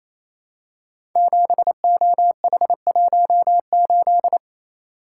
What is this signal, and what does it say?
Morse code, heard as a keyed tone, about 28 wpm, 700 Hz. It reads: 7OH18
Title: Code Groups - 5 Letter:Numbers - 28wpm